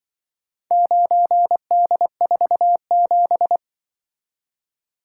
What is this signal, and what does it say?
Morse code, heard as a keyed tone, about 26 wpm, 700 Hz. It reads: 9D47